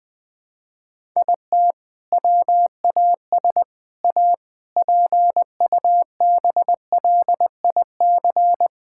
Morse code, heard as a keyed tone, about 20 wpm, 700 Hz. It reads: IT WAS A PUBLIC
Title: Sentences from Top 1600 Words 20wpm